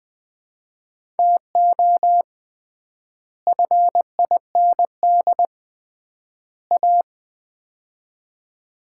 Morse code, heard as a keyed tone, about 21 wpm, 700 Hz. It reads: TO FIND A
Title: 3x Word-Spacing - Sets of 3 Words - 20wpm